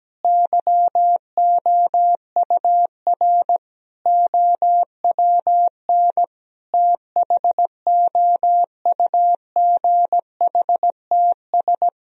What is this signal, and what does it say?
Morse code, heard as a keyed tone, about 17 wpm, 700 Hz. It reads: YOUR OWN THOUGHTS
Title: Sentences from Top 1000 Words - Review Part 1 - 17wpm